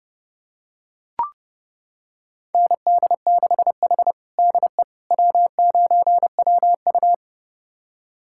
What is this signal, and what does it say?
Morse code, heard as a keyed tone, about 32 wpm, 700 Hz. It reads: ND6H DE W9WU
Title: USA Callsign DE Callsign - 4 Characters - 30wpm